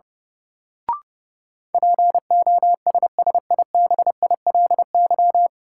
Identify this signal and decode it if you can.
Morse code, heard as a keyed tone, about 32 wpm, 700 Hz. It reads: POSSIBILY
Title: Sentences from Top 1300 Words - New Word Drill - 30wpm